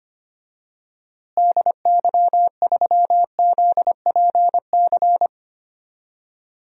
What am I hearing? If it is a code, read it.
Morse code, heard as a keyed tone, about 25 wpm, 700 Hz. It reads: DY3ZPC
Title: DXCC-100 Call Signs - 6 Characters - Repeated 3x - 25wpm